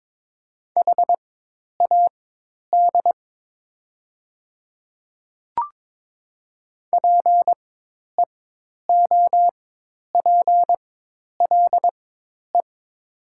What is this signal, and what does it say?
Morse code, heard as a keyed tone, about 22 wpm, spaced as for 10 wpm, 700 Hz. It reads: HAD PEOPLE
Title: 4x Character Spacing - Top 200 Words - 22wpm